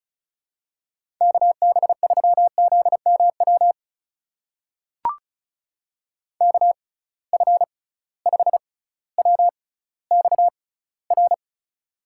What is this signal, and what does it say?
Morse code, heard as a keyed tone, about 34 wpm, spaced as for 18 wpm, 700 Hz. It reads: KB3ZMW KF5WXR
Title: USA Call Signs - 6 Characters - 6x Character Spacing - 35wpm